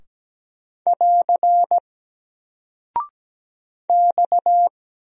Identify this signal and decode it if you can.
Morse code, heard as a keyed tone, about 17 wpm, 700 Hz. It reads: <AR> X
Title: Single Character (Letter:Number:Punctuation:Prosign) Alt 17wpm